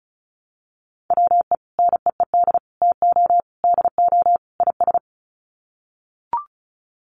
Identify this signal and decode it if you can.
Morse code, heard as a keyed tone, about 34 wpm, 700 Hz. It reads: WE NEED TO DO IS
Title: Sets of 5 Words Encore2 35wpm